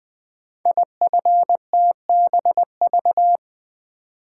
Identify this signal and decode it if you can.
Morse code, heard as a keyed tone, about 20 wpm, 700 Hz. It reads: IFTBV
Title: Code Groups - 5 Letters - 20wpm